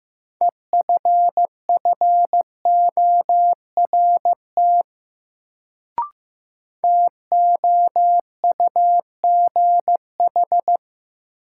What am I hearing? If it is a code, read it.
Morse code, heard as a keyed tone, about 15 wpm, 700 Hz. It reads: EFFORT TOUGH
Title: Sentences from Top 1200 Words - New Word Drill Alt - 15wpm